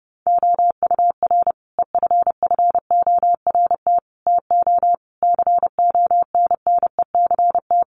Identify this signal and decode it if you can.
Morse code, heard as a keyed tone, about 30 wpm, 700 Hz. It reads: OUR EFFORT TO CONNECT
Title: Sentences from Top 1500 Words 30wpm